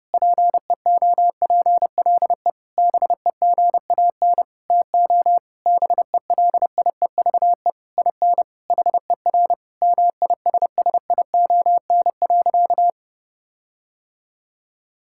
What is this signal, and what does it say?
Morse code, heard as a keyed tone, about 30 wpm, 700 Hz. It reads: PEOPLE BEGAN TO BELIEVE IN HER MISSION.
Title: Sentences from Top 1600 Words 30wpm